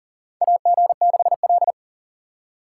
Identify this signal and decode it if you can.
Morse code, heard as a keyed tone, about 40 wpm, 700 Hz. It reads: AG6L
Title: USA Call Signs - 4 Characters - 4x Character Spacing - 40wpm